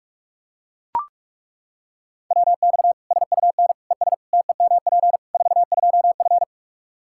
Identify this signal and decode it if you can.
Morse code, heard as a keyed tone, about 41 wpm, 700 Hz. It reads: WX SUN ES TEMP 42F
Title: K1USN SST QSO Practice 45wpm